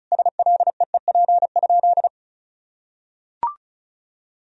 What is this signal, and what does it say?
Morse code, heard as a keyed tone, about 36 wpm, 700 Hz. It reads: SLEEP?
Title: Sentences from Top 500 Words and Abbreviations 35wpm